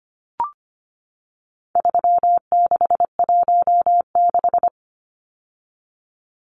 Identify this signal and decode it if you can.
Morse code, heard as a keyed tone, about 25 wpm, 700 Hz. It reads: 3616
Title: Numbers - 4-Digit 25wpm